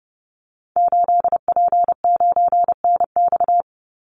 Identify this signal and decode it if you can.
Morse code, heard as a keyed tone, about 30 wpm, 700 Hz. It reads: 8P9NX
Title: Call Signs Repeated 2x - 30wpm